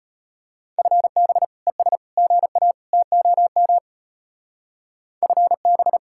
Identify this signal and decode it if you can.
Morse code, heard as a keyed tone, about 32 wpm, 700 Hz. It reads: FB ES GA TOM FB